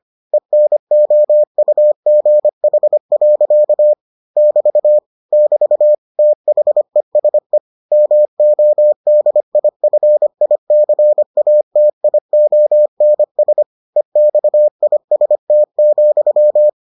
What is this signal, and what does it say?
Morse code, heard as a keyed tone, about 25 wpm, 600 Hz. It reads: ENOUGH. <BT> <BT> THESE MODIFICATIONS EXIST,